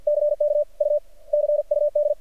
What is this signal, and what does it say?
Morse code, heard as a keyed tone, about 48 wpm, 590 Hz. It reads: CK U CUD
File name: archived / CRSnet